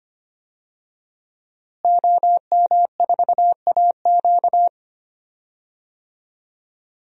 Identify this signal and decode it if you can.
Morse code, heard as a keyed tone, about 25 wpm, 700 Hz. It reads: OM4AQ
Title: DXCC-100 Call Signs - 5 Characters - Repeated 3x - 25wpm